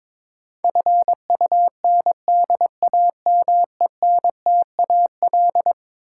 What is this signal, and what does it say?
Morse code, heard as a keyed tone, about 22 wpm, 700 Hz. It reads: FUNDAMENTAL
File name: practice / 22wpm / Eleven Letter Words - 22wpm